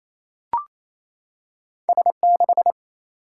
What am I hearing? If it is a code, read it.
Morse code, heard as a keyed tone, about 28 wpm, 700 Hz. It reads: S 6
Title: Single-Letter Single-Number 28wpm